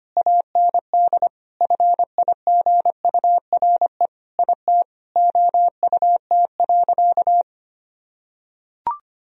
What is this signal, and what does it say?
Morse code, heard as a keyed tone, about 25 wpm, 700 Hz. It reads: AND FIGURE IT OUT.
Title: Sentences from Top 1000 Words - Review Part 2 - 25wpm